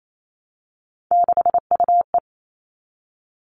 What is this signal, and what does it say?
Morse code, heard as a keyed tone, about 28 wpm, 700 Hz. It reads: 6UE
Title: All Cut Numbers - 3-Digit 28wpm